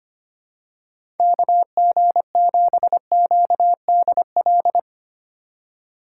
Morse code, heard as a keyed tone, about 25 wpm, 700 Hz. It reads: KG7QDL
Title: USA Call Signs - 6 Characters - 4x Character Spacing - 25wpm